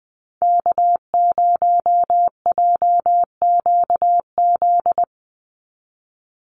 Morse code, heard as a keyed tone, about 20 wpm, 700 Hz. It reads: K0JQZ
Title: NA SOTA Activator Callsigns 20wpm